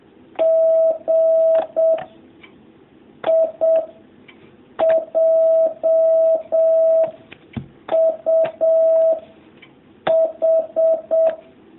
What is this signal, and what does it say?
Morse code, heard as a keyed tone, about 7 wpm, 650 Hz. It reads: G I J U H
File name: jh8cbh / 2019cw